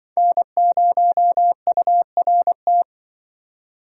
Morse code, heard as a keyed tone, about 25 wpm, 700 Hz. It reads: N0URT
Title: Speed Racing - USA Call Signs 20wpm